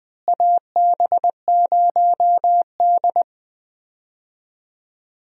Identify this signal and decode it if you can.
Morse code, heard as a keyed tone, about 20 wpm, 700 Hz. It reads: AB0D
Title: USA Call Signs - 4 Characters - Repeated 2x - 20wpm